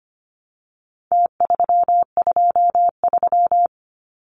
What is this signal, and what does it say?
Morse code, heard as a keyed tone, about 25 wpm, 700 Hz. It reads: T323
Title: Numbers - Contest Style - 4-Digit 25wpm